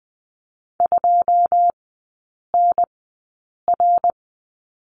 Morse code, heard as a keyed tone, about 20 wpm, spaced as for 8 wpm, 700 Hz. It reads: 2NR
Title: 159 Lesson 26e - Letter g - Three Characters 20wpm